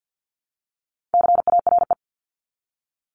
Morse code, heard as a keyed tone, about 50 wpm, 700 Hz. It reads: CARE
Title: Multi-letter Phonemes 50wpm